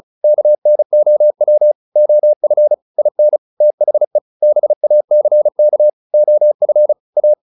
Morse code, heard as a keyed tone, about 34 wpm, 600 Hz. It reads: KNOW OF IN THE BACK OF A
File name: WD403 35